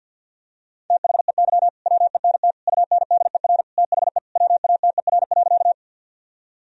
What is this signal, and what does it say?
Morse code, heard as a keyed tone, about 46 wpm, 700 Hz. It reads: THEY WENT UNDER THE WATER.
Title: Sentences from Top 1000 Words - Review Part 1 - 50wpm